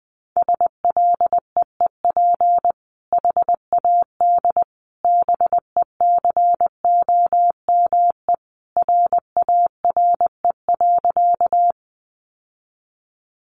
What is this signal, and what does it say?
Morse code, heard as a keyed tone, about 20 wpm, 700 Hz. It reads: SLEEP HAD BECOME RARE.
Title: Sentences from Top 1100 Words 20wpm